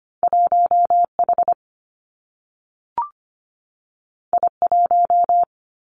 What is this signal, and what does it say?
Morse code, heard as a keyed tone, about 25 wpm, 700 Hz. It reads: 1H I1
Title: Two Challenging Letter-Number Combinations 25wpm